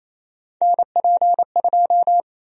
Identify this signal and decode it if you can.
Morse code, heard as a keyed tone, about 28 wpm, 700 Hz. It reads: NP2
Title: USA Call Sign Prefixes - 28wpm